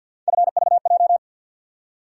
Morse code, heard as a keyed tone, about 52 wpm, 700 Hz. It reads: FVJ
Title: USA Call Sign Postfixes - 3 Letters 50wpm